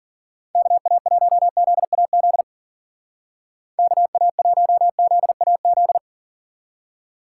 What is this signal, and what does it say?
Morse code, heard as a keyed tone, about 52 wpm, 700 Hz. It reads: KA1ZAZ KA1ZAZ